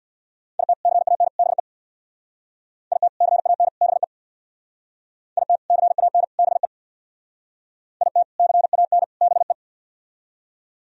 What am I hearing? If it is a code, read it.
Morse code, heard as a keyed tone, about 63 wpm, 700 Hz. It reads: IT CAN BE IT CAN BE IT CAN BE IT CAN BE